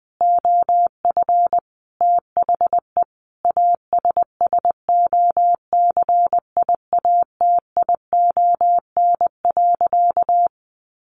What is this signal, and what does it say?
Morse code, heard as a keyed tone, about 20 wpm, 700 Hz. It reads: OF THE ASSOCIATION.
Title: Sentences from Top 1200 Words 20wpm